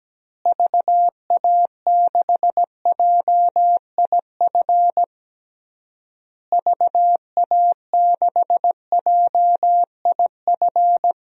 Morse code, heard as a keyed tone, about 17 wpm, 700 Hz. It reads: VA6JIF VA6JIF
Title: DXCC-100 Call Signs - 6 Characters - Repeated 3x - 17wpm